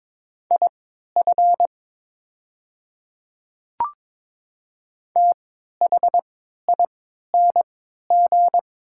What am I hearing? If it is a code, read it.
Morse code, heard as a keyed tone, about 22 wpm, spaced as for 12 wpm, 700 Hz. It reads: IF THING
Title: 3x Character Spacing - Top 200 Words - 22wpm